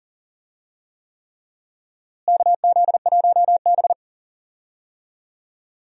Morse code, heard as a keyed tone, about 38 wpm, 700 Hz. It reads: KZ1B
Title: USA Call Signs - 4 Characters - Repeated 3x - 40wpm